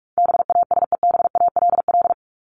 Morse code, heard as a keyed tone, about 44 wpm, 700 Hz. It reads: BASEBALL
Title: Top 1751-2000 Words 45wpm